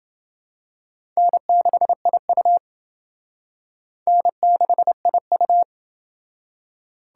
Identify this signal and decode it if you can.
Morse code, heard as a keyed tone, about 28 wpm, 700 Hz. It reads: N6IU N6IU